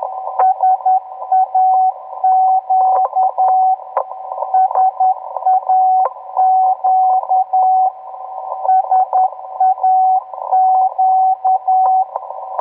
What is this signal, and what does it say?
Morse code, heard as a keyed tone, about 10 wpm, 750 Hz, with radio noise. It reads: SAQ SAQ SAQ